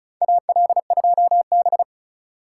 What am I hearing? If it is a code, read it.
Morse code, heard as a keyed tone, about 34 wpm, 700 Hz. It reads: AL2B
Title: USA Call Signs - 4 Characters - 2x Character Spacing - 35wpm